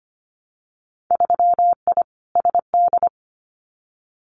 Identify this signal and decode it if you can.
Morse code, heard as a keyed tone, about 25 wpm, 690 Hz. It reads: 3I SD
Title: Winter Field Day Exchanges Encore 25wpm